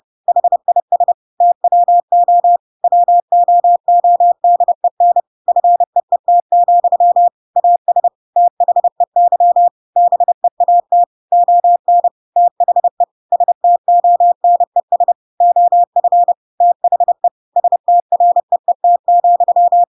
Morse code, heard as a keyed tone, about 30 wpm, 700 Hz. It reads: HIS TWO WOODEN FEET, AS THEY BEAT ON THE STONES OF THE STREET,
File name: Pinocchio / Pinocchio ch3 30 30 ik1hge